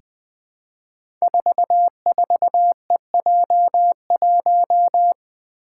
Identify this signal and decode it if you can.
Morse code, heard as a keyed tone, about 20 wpm, 700 Hz. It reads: 44EJ1
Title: Code Groups - 5 Letter:Numbers - 20wpm